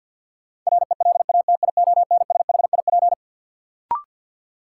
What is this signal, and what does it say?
Morse code, heard as a keyed tone, about 52 wpm, 700 Hz. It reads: RELATIONSHIP
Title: Sentences from Top 1100 Words - New Word Drill Alt - 50wpm